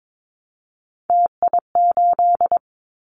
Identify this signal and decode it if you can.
Morse code, heard as a keyed tone, about 22 wpm, 700 Hz. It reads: TI8
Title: Callsign Prefixes for Top 100 DXCC - 22wpm